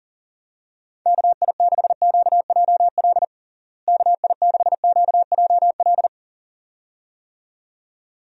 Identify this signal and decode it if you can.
Morse code, heard as a keyed tone, about 40 wpm, 700 Hz. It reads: KI6QJL KI6QJL
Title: USA Call Signs - 6 Characters - Repeated 2x - 40wpm